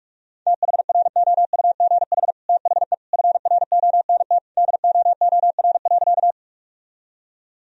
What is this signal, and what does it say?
Morse code, heard as a keyed tone, about 46 wpm, 700 Hz. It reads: THROUGH THE FRONT DOOR.
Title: Sentences from Top 1000 Words - Review Part 2 - 45wpm